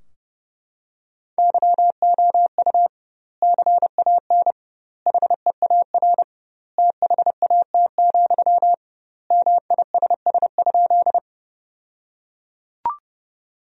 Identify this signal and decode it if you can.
Morse code, heard as a keyed tone, about 30 wpm, 700 Hz. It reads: YOU CAN HEAR THAT, MISS?
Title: 2x Word-Spacing - Sentences from Top 500 Words - 30wpm